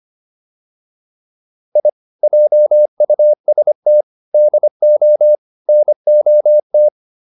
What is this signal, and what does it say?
Morse code, heard as a keyed tone, about 25 wpm, 600 Hz. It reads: I JUST DO NOT